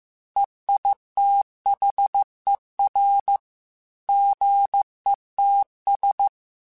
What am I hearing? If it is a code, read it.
Morse code, heard as a keyed tone, about 15 wpm, 800 Hz. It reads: EITHER GETS
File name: Chuckles / Chuckles-mp3s / Squeaky @13